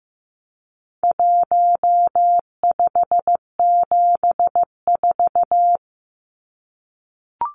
Morse code, heard as a keyed tone, about 15 wpm, 700 Hz. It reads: 1574
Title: Numbers - 4-Digit 15wpm